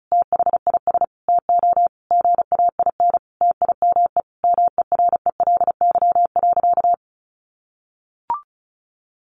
Morse code, heard as a keyed tone, about 34 wpm, 700 Hz. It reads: THIS TO GAIN TIME MERELY.
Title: Sentences from Top 1400 Words 35wpm